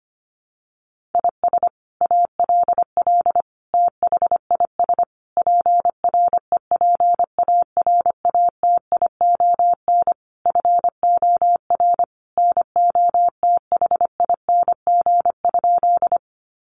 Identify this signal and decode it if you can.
Morse code, heard as a keyed tone, about 25 wpm, 700 Hz. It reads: IS ALL THIS PREPARATION FOR NOTHING?
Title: Sentences from Top 1400 Words 25wpm